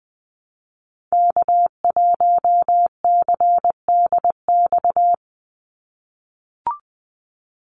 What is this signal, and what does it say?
Morse code, heard as a keyed tone, about 20 wpm, 700 Hz. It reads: K1CDX
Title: Call Signs - 20wpm